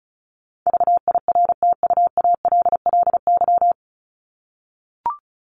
Morse code, heard as a keyed tone, about 34 wpm, 700 Hz. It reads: VIRTUALLY
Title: LY Suffix - Top 100 Words with Base Words 35wpm